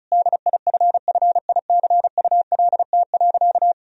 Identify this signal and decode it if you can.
Morse code, heard as a keyed tone, about 36 wpm, 700 Hz. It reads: DIFFICULT.
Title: Sentences from Top 1000 Words - Review Part 2 - 35wpm